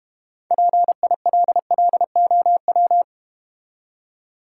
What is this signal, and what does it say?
Morse code, heard as a keyed tone, about 32 wpm, 700 Hz. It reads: PILLOW